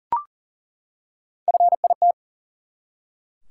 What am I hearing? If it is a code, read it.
Morse code, heard as a keyed tone, about 38 wpm, 700 Hz. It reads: FIT
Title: Three Letter Words - 40wpm